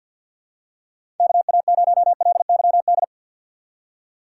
Morse code, heard as a keyed tone, about 50 wpm, 700 Hz. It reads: KA0LYD
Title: USA Call Signs - 6 Characters - 3x Character Spacing - 50wpm